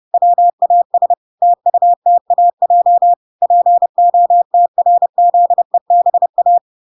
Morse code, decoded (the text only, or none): WAS TUTAJ POTRZEBA